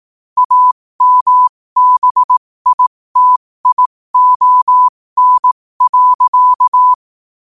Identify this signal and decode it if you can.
Morse code, heard as a keyed tone, about 18 wpm, spaced as for 15 wpm, 1000 Hz. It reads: AMBITION.
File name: qotdfiles / 15wpm